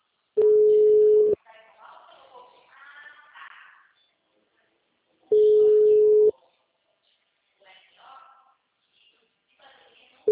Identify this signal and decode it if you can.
Ringback tone (Europe, 425 Hz), with speech.